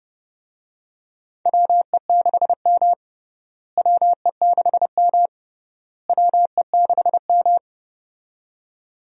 Morse code, heard as a keyed tone, about 30 wpm, 700 Hz. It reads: WE6M WE6M WE6M
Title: USA Call Signs - 4 Characters - Repeated 3x - 30wpm